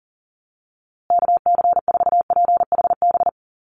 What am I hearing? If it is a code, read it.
Morse code, heard as a keyed tone, about 40 wpm, 700 Hz. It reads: KC4PHB
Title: USA Call Signs Repeated 2x 40wpm